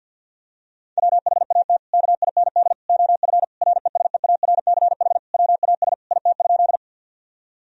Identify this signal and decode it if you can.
Morse code, heard as a keyed tone, about 50 wpm, 700 Hz. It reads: WHAT KIND OF RESEARCH WAS IT?
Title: Sentences from Top 1000 Words - Review Part 4 - 50wpm